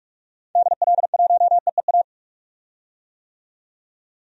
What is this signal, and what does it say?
Morse code, heard as a keyed tone, about 46 wpm, 700 Hz. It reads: DL1EEA
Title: DXCC-100 Call Signs - 6 Characters - 45wpm